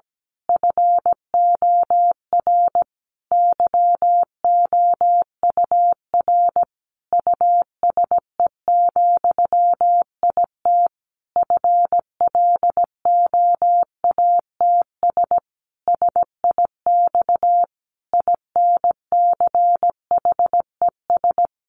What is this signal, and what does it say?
Morse code, heard as a keyed tone, about 17 wpm, 700 Hz. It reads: FOR YOUR USE, IT FLOATS SIX INCHES
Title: A Sound Of Thunder 17wpm